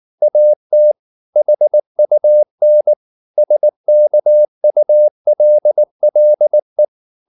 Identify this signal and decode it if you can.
Morse code, heard as a keyed tone, about 19 wpm, 600 Hz. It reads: AT HUN SKULLE
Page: Ferie 19 wpm